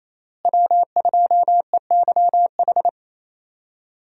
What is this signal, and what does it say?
Morse code, heard as a keyed tone, about 28 wpm, 700 Hz. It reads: W2EYH
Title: USA Call Signs - 5 Characters - Repeated 2x - 28wpm